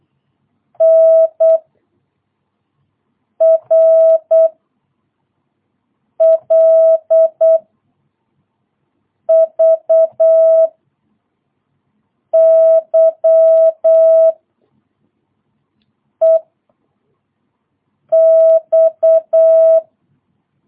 Morse code, heard as a keyed tone, about 8 wpm, 650 Hz. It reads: N R L V Y E X
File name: jh8cbh / 2022cw